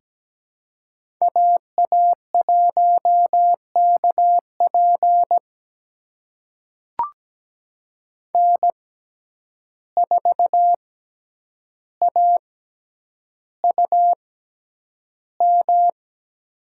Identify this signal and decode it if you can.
Morse code, heard as a keyed tone, about 17 wpm, spaced as for 9 wpm, 700 Hz. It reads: AA1KP N4AUM
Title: USA Call Signs - 5 Characters - 6x Character Spacing - 17wpm